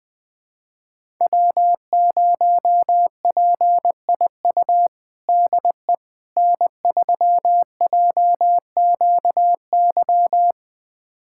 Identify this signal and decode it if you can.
Morse code, heard as a keyed tone, about 21 wpm, 700 Hz. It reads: W0PIU DE N3JQY
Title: USA Callsign DE Callsign - 5 Characters - 20wpm